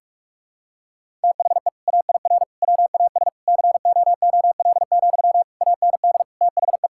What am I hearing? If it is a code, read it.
Morse code, heard as a keyed tone, about 46 wpm, 700 Hz. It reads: THE AIR WAS COOL, AND THE
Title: Sentences from Top 1600 Words 45wpm